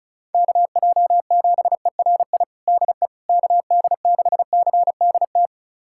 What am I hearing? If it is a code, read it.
Morse code, heard as a keyed tone, about 36 wpm, 700 Hz. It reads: KJ7ERI DE KD6CDT
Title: USA Callsign DE Callsign 35wpm